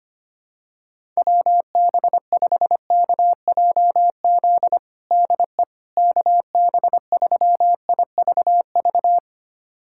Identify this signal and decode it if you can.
Morse code, heard as a keyed tone, about 25 wpm, 700 Hz. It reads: WB5KJZ DE KB3IVV
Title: USA Callsign DE Callsign - 6 Characters - 25wpm